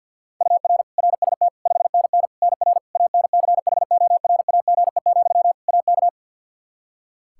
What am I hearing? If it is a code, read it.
Morse code, heard as a keyed tone, about 47 wpm, 700 Hz. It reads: UR RST 5NN NR ANCHORAGE, AK
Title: Practice QSO - Exchange 1 - Beginning - 50wpm